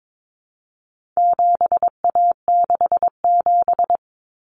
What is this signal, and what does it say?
Morse code, heard as a keyed tone, about 22 wpm, 700 Hz. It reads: 7A67
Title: Common Cut Numbers - 4-Digit 22wpm